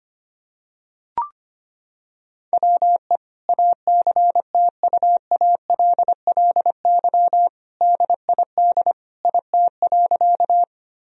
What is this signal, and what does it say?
Morse code, heard as a keyed tone, about 25 wpm, 700 Hz. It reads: WE ACTUALLY DID IT.
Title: Sentences from Top 1000 Words - Review Part 4 - 25wpm